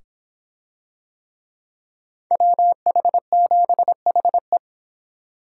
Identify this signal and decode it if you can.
Morse code, heard as a keyed tone, about 25 wpm, 700 Hz. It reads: WH7HE